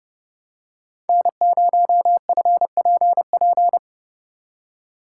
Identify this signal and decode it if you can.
Morse code, heard as a keyed tone, about 32 wpm, 700 Hz. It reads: N0FPP
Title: USA Call Signs - 5 Characters - 6x Character Spacing - 30wpm